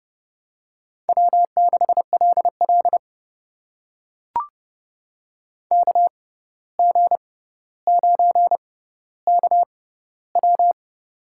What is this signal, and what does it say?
Morse code, heard as a keyed tone, about 30 wpm, spaced as for 14 wpm, 700 Hz. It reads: W6LL KG9KW
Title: USA Call Signs - 6x Character Spacing - 30wpm